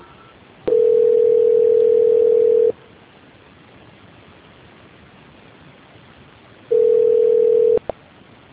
Ringback tone (North America), 440 + 480 Hz.
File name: out- -2026- - - .23279